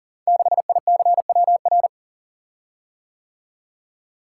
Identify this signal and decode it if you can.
Morse code, heard as a keyed tone, about 39 wpm, 700 Hz. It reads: 6ICWR
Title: Code Groups - 5 Letter:Numbers - 40wpm